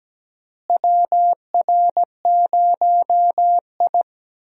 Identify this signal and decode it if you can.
Morse code, heard as a keyed tone, about 17 wpm, 700 Hz. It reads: WR0I
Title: USA Call Signs - 4 Characters - Repeated 3x - 17wpm